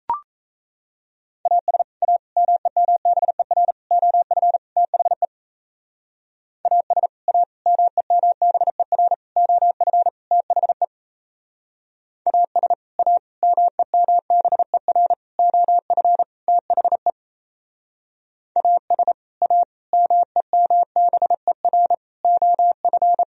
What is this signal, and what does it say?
Morse code, heard as a keyed tone, about 38 wpm, 700 Hz. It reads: AS A MEMBER OF THE AS A MEMBER OF THE AS A MEMBER OF THE AS A MEMBER OF